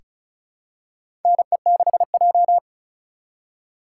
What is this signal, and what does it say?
Morse code, heard as a keyed tone, about 36 wpm, 700 Hz. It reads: NE6J
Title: USA Call Signs - 4 Characters - 3x Character Spacing - 35wpm